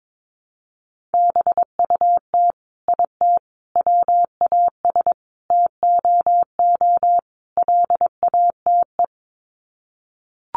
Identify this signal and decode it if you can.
Morse code, heard as a keyed tone, about 22 wpm, 700 Hz. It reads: BUT IT WAS TOO LATE
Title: Sets of 5 Words Encore2 22wpm